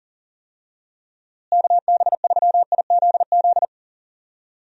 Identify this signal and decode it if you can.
Morse code, heard as a keyed tone, about 40 wpm, 700 Hz. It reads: KB3IZZ
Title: USA Call Signs - 6 Characters - Repeated 2x - 40wpm